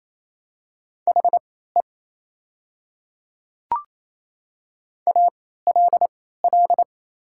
Morse code, heard as a keyed tone, about 28 wpm, spaced as for 16 wpm, 700 Hz. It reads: HE ALL
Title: 3x Character Spacing - Top 100 Words - 28wpm